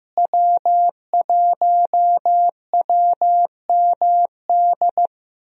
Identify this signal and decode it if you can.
Morse code, heard as a keyed tone, about 16 wpm, 700 Hz. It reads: W1WMD
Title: USA Call Signs - 5 Characters - Repeated 3x - 15wpm